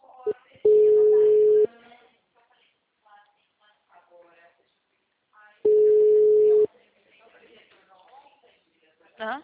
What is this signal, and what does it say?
Ringback tone (Europe, 425 Hz), with speech.